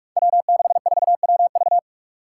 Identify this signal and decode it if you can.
Morse code, heard as a keyed tone, about 44 wpm, 700 Hz. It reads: W64WV
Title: Five Challenging Letter-Number Combinations 45wpm